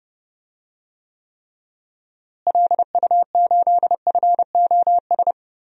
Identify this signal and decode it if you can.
Morse code, heard as a keyed tone, about 32 wpm, 700 Hz. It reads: LU8FOS
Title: DXCC-100 Call Signs - 6 Characters - Repeated 3x - 30wpm